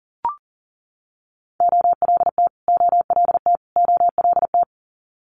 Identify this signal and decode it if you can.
Morse code, heard as a keyed tone, about 41 wpm, 700 Hz. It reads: OLT OLT OLT
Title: Multi-letter Phonemes Repeated 3x 40wpm